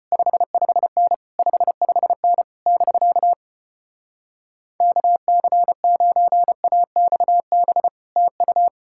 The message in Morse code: 55N 55N <BK> KC9AXB TU

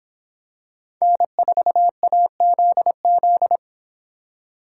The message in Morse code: N4AZZ

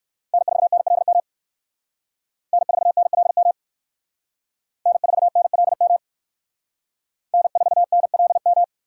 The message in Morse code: N4NLM N4NLM N4NLM N4NLM

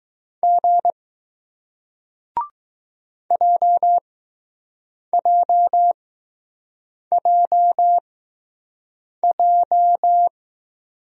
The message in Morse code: G JJJJ

